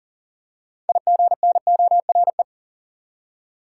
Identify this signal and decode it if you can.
Morse code, heard as a keyed tone, about 40 wpm, 700 Hz. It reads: IGNORE
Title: Six Letter Words - 40wpm